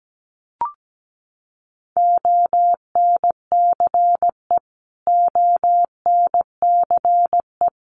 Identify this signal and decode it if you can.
Morse code, heard as a keyed tone, about 17 wpm, 700 Hz. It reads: ONCE ONCE
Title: Top 500 Words Repeated 17wpm